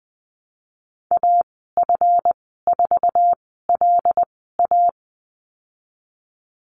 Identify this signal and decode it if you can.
Morse code, heard as a keyed tone, about 20 wpm, spaced as for 15 wpm, 700 Hz. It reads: AF4LA
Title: 173 Lesson 28g - Number 9 - Call Signs 20wpm